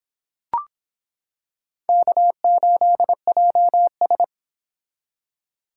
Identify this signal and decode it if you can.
Morse code, heard as a keyed tone, about 26 wpm, 700 Hz. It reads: K8JS